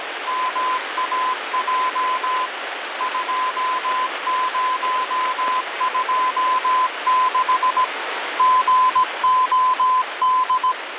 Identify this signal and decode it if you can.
Morse code (Russian alphabet), heard as a keyed tone, about 18 wpm, 1020 Hz, with radio noise. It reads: МАЙ 2026 ГОД
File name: original